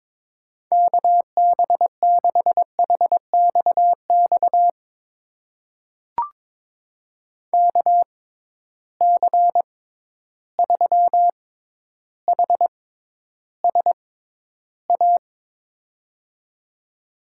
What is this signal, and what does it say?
Morse code, heard as a keyed tone, about 22 wpm, spaced as for 11 wpm, 700 Hz. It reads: KB6HXX KC3HSA
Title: USA Call Signs - 6 Characters - 6x Character Spacing - 22wpm